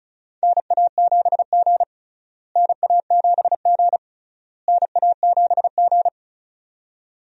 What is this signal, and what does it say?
Morse code, heard as a keyed tone, about 34 wpm, 700 Hz. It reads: NA7G NA7G NA7G
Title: USA Call Signs - 4 Characters - Repeated 3x - 35wpm